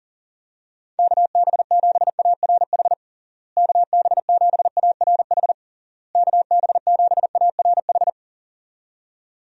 Morse code, heard as a keyed tone, about 40 wpm, 700 Hz. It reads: KB7ARH KB7ARH KB7ARH
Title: USA Call Signs - 6 Characters - Repeated 3x - 40wpm